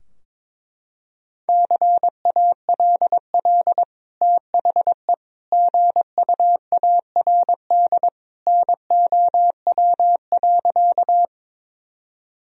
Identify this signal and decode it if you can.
Morse code, heard as a keyed tone, about 22 wpm, 700 Hz. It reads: CALL THE GUARD NOW.
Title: Sentences from Top 1000 Words - Review Part 4 - 22wpm